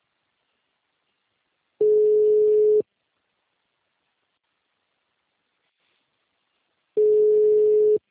Ringback tone (Europe, 425 Hz).